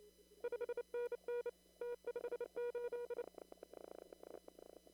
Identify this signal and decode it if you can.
Morse code, heard as a keyed tone, about 28 wpm, 480 Hz. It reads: 5NN T58
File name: CW / kt5j wpx cw-op-k5pi